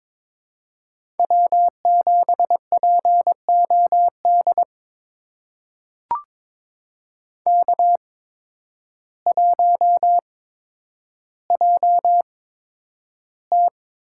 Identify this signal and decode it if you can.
Morse code, heard as a keyed tone, about 22 wpm, spaced as for 10 wpm, 700 Hz. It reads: W7POD K1JT
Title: USA Call Signs - 8x Character Spacing - 22wpm